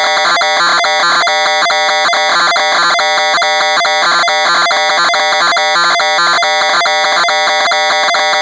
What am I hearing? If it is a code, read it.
Two-tone teleprinter signal, about 420 Hz shift.